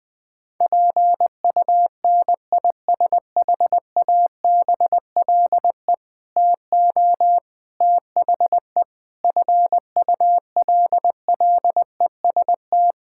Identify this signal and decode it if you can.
Morse code, heard as a keyed tone, about 19 wpm, 700 Hz. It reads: PUNISHABLE TO THE FULLEST
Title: Sentences with 7-Word Phrases - Spoken Beginning - 20wpm